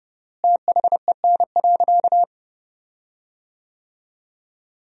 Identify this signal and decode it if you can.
Morse code, heard as a keyed tone, about 30 wpm, 700 Hz. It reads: THEN.
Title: 4x Word-Spacing - Sentences from Top 200 Words - 30wpm